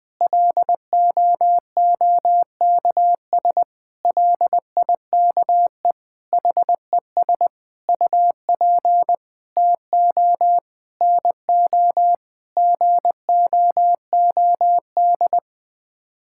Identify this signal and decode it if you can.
Morse code, heard as a keyed tone, about 21 wpm, 700 Hz. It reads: LOOKS LIKE HES UP TO NO GOOD
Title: Sets of 9 Words - 20wpm